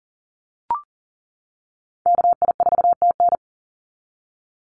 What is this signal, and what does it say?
Morse code, heard as a keyed tone, about 40 wpm, 700 Hz. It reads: KI4TN
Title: NA SOTA Chaser Callsigns 40wpm